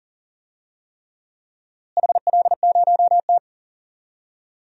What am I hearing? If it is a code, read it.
Morse code, heard as a keyed tone, about 40 wpm, 700 Hz. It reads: HL0T
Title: DXCC-100 Call Signs - 4 Characters - Repeated 3x - 40wpm